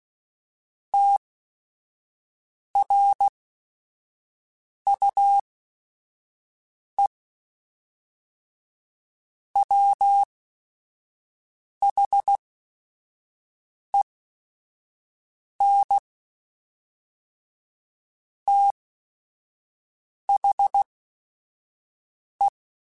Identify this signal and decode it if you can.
Morse code, heard as a keyed tone, about 16 wpm, spaced as for 5 wpm, 800 Hz. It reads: TRUE WHEN THE
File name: Technical Subjects / Technical Subjects-mp3s / AND Gate @5